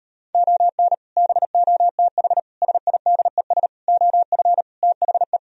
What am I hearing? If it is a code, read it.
Morse code, heard as a keyed tone, about 39 wpm, 700 Hz. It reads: ON BOTH SIDES OF THE